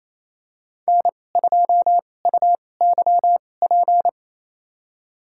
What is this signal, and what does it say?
Morse code, heard as a keyed tone, about 28 wpm, spaced as for 20 wpm, 700 Hz. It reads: N2UYP
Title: USA Call Signs - 5 Characters - 2x Character Spacing - 28wpm